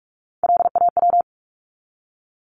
Morse code, heard as a keyed tone, about 44 wpm, 700 Hz. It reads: LAW
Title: Top 501-750 Words 45wpm